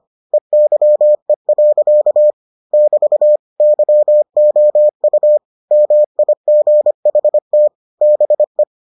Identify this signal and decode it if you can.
Morse code, heard as a keyed tone, about 25 wpm, 600 Hz. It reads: EYE. <BT> YOU MIGHT BE